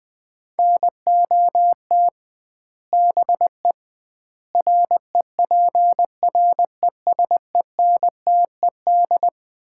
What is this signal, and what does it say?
Morse code, heard as a keyed tone, about 20 wpm, 700 Hz. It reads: NOT BE REPRESENTED
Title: 2x Word-Spacing - Sets of 5 Words - 20wpm